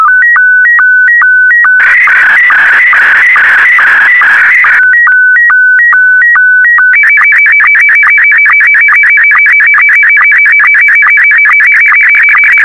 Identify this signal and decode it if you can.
SSTV (slow-scan television), Scottie 1.